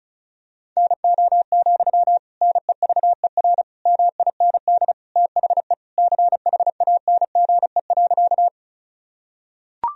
Morse code, heard as a keyed tone, about 34 wpm, 700 Hz. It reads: NO, NEVER MIND THE CHANGE.
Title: Sentences from Top 1000 Words - Review Part 1 - 35wpm